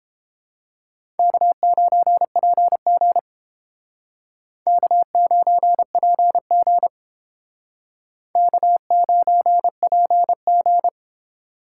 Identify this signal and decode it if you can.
Morse code, heard as a keyed tone, about 29 wpm, 700 Hz. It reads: K9PG K9PG K9PG